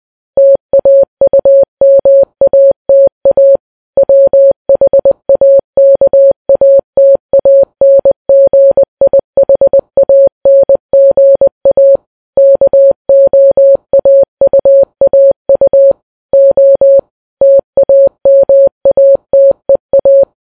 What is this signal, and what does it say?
Morse code, heard as a keyed tone, about 20 wpm, 550 Hz. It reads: TAUMATA WHAKATANGIHANGA KOAUAU O TAMATEA